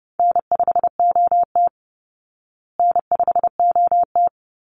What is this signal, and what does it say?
Morse code, heard as a keyed tone, about 30 wpm, 700 Hz. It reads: N5OT N5OT
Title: Easy Call Signs Repeated 2x - 30wpm